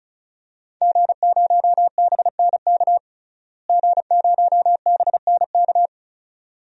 Morse code, heard as a keyed tone, about 35 wpm, 700 Hz. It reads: G0BNK G0BNK
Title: DXCC-100 Call Signs - 5 Characters - Repeated 2x - 35wpm